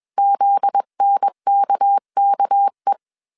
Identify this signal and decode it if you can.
Morse code, heard as a keyed tone, about 21 wpm, 800 Hz. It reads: ZNKKE